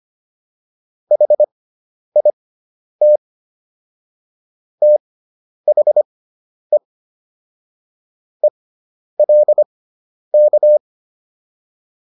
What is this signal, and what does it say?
Morse code, heard as a keyed tone, about 25 wpm, spaced as for 10 wpm, 600 Hz. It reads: HIT THE ELK